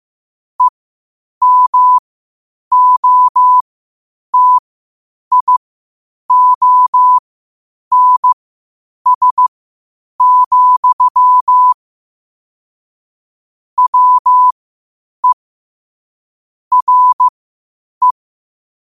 Morse code, heard as a keyed tone, about 15 wpm, spaced as for 8 wpm, 1000 Hz. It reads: EMOTIONS, WE RE